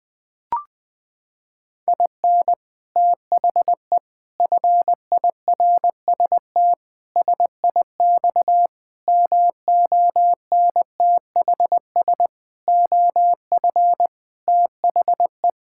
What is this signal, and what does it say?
Morse code, heard as a keyed tone, about 21 wpm, 700 Hz. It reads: IN THE FIRST SIX MONTHS OF THE
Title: Sentences with 6-Word Phrases - 20wpm